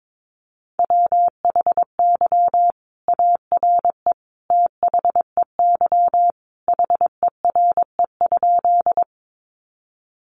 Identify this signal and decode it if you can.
Morse code, heard as a keyed tone, about 22 wpm, 700 Hz. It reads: WHY ARE THEY HERE?
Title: Sentences from Top 200 Words 22wpm